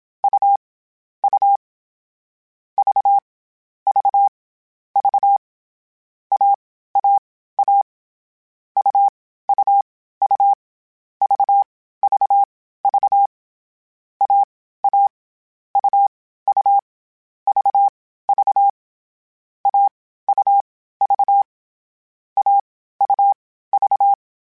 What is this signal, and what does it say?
Morse code, heard as a keyed tone, about 27 wpm, spaced as for 15 wpm, 800 Hz. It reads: U U V V V AAA UUU VVV AA UU VV AUV AUV
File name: Difficult Combos / Difficult Combos-mp3s / AUV ShortRun @25